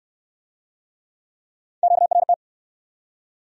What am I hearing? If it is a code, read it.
Morse code, heard as a keyed tone, about 68 wpm, 700 Hz. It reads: QRT